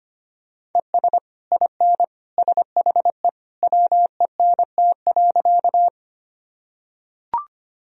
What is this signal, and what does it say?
Morse code, heard as a keyed tone, about 25 wpm, 700 Hz. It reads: ES IN SHE WENT.
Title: Sentences from Top 200 Words and Abbreviations 25wpm